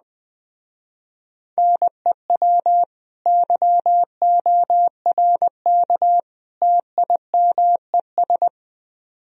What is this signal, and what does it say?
Morse code, heard as a keyed tone, about 21 wpm, 700 Hz. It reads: NEW YORK TIMES